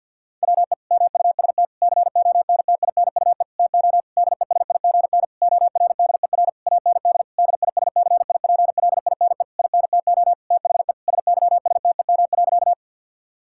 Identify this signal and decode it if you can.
Morse code, heard as a keyed tone, about 50 wpm, 700 Hz. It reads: WE MUST CONTINUE TO DESIGN ORDER AND DISCIPLINE INTO THE SYSTEM.
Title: Sentences from Top 1000 Words - Review Part 6 - 50wpm